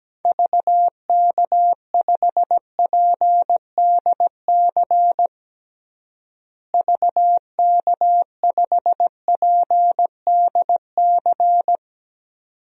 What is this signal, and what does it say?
Morse code, heard as a keyed tone, about 17 wpm, 700 Hz. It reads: VK5PDC VK5PDC
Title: DXCC-100 Call Signs - 6 Characters - Repeated 2x - 17wpm